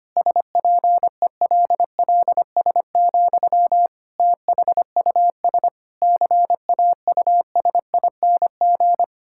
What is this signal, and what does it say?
Morse code, heard as a keyed tone, about 25 wpm, 700 Hz. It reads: SPELLS, THUS CAUSING
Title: Pangram Challenge v2 - 25wpm